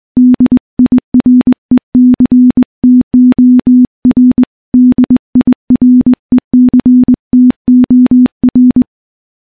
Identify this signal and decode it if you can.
Morse code, heard as a keyed tone, about 21 wpm, 250 Hz. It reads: DIRECTOR DIRECTOR
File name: morse-training-20WPM